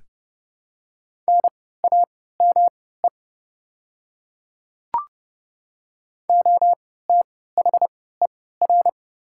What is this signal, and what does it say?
Morse code, heard as a keyed tone, about 29 wpm, spaced as for 17 wpm, 690 Hz. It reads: NAME OTHER
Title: 3x Character Spacing - Top 200 Words - 30wpm